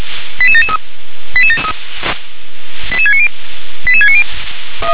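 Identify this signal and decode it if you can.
Five-tone selective call (ZVEI), heard as quick runs of five tones, tones of about 70 ms. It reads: 88773 88690 88690
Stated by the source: SDR-received ZVEI five-tone selective call.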